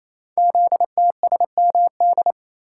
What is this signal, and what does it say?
Morse code, heard as a keyed tone, about 27 wpm, 700 Hz. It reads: ZTSMD
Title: Code Groups - 5 Letters - 28wpm